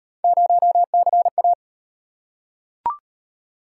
Morse code, heard as a keyed tone, about 38 wpm, 700 Hz. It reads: 0CA